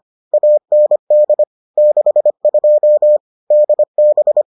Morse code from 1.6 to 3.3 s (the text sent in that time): 62